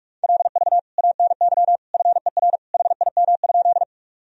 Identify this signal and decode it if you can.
Morse code, heard as a keyed tone, about 41 wpm, 700 Hz. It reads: LV ANY FER HIM?
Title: Sentences from Top 400 Words and Abbreviations 45wpm